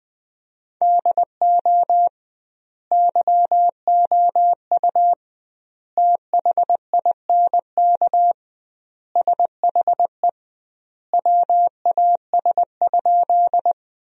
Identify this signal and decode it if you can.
Morse code, heard as a keyed tone, about 19 wpm, 700 Hz. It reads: DO YOU THINK SHE WAS?
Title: 2x Word-Spacing - Sentences from Top 200 Words - 20wpm